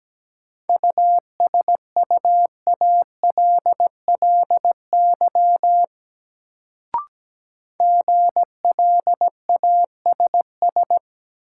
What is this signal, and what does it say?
Morse code, heard as a keyed tone, about 17 wpm, 700 Hz. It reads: USUALLY GLASS
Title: Sentences from Top 1000 Words - Review Part 3 - New Word Drill - 17wpm